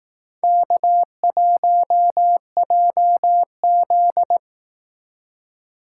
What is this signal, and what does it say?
Morse code, heard as a keyed tone, about 18 wpm, 700 Hz. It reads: K1JZ
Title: Speed Racing - USA 1x2 Call Signs 15wpm